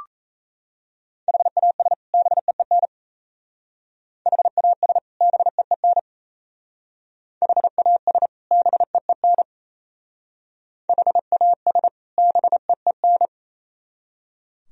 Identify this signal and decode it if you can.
Morse code, heard as a keyed tone, about 34 wpm, 700 Hz. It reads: HAS BEEN HAS BEEN HAS BEEN HAS BEEN
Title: Speed-Racing - Sets of 2 Words 28wpm